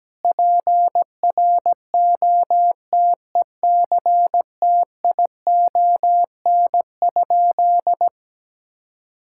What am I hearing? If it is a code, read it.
Morse code, heard as a keyed tone, about 17 wpm, 700 Hz. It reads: PROTECTION?
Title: Sentences from Top 1600 Words 17wpm